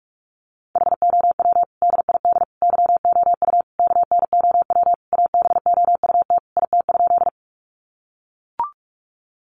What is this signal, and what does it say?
Morse code, heard as a keyed tone, about 44 wpm, 700 Hz. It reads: HOW DID YOU KNOW ABOUT IT?
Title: Sentences from Top 100 Words 45wpm